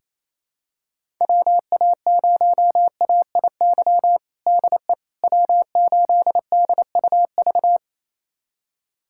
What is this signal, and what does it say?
Morse code, heard as a keyed tone, about 28 wpm, 700 Hz. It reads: WA0AIY DE W8DUV
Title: USA Callsign DE Callsign 28wpm